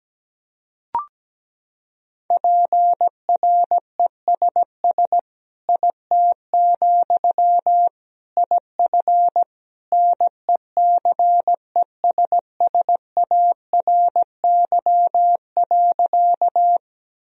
Morse code, heard as a keyed tone, about 17 wpm, 700 Hz. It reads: PRESS IT, IF NECESSARY.
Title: Sentences from Top 1000 Words - Review Part 1 - 17wpm